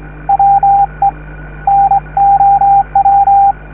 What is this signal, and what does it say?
Morse code, heard as a keyed tone, about 22 wpm, 790 Hz, with radio noise. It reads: WE NOW